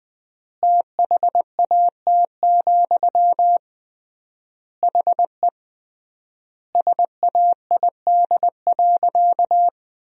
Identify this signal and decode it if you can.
Morse code, heard as a keyed tone, about 21 wpm, 700 Hz. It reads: THAT, HE SAID.
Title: 3x Word-Spacing - Sentences from Top 200 Words - 20wpm